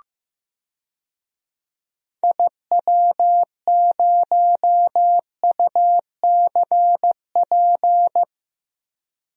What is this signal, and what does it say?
Morse code, heard as a keyed tone, about 15 wpm, 700 Hz. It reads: IW0UCP
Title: DXCC-100 Call Signs - 6 Characters - Repeated 3x - 15wpm